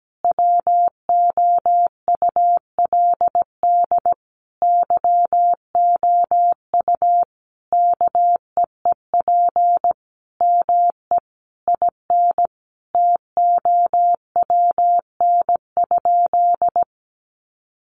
Morse code, heard as a keyed tone, about 17 wpm, 700 Hz. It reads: WOULD YOU KEEP ME IN TOWN?
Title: Sentences from Top 500 Words 17wpm